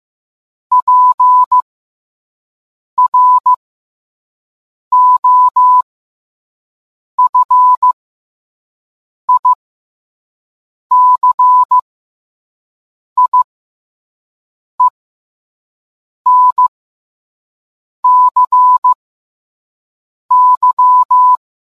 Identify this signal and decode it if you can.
Morse code, heard as a keyed tone, about 15 wpm, spaced as for 5 wpm, 1000 Hz. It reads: PROFICIENCY